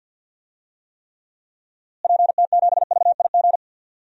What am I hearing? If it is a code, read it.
Morse code, heard as a keyed tone, about 52 wpm, 700 Hz. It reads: PT7VIG
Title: DXCC-100 Call Signs - 6 Characters - Repeated 3x - 50wpm